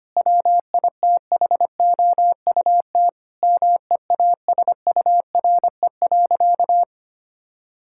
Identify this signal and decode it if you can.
Morse code, heard as a keyed tone, about 25 wpm, 700 Hz. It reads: WITHOUT MEASURE.
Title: Sentences from Top 1000 Words - Review Part 6 - 25wpm